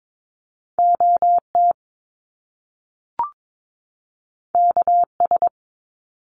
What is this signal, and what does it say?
Morse code, heard as a keyed tone, about 22 wpm, 700 Hz. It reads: OT KS
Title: Top Two Letter Combinations in Words 22wpm